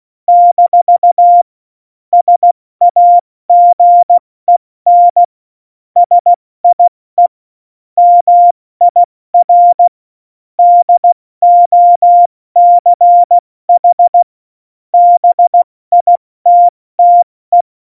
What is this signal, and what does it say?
Morse code, heard as a keyed tone, about 16 wpm, 700 Hz. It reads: - SAGEN SIE MIR DOCH BITTE
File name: Morsekurs / Witze / Witz 14 Juli 2020, 70 F80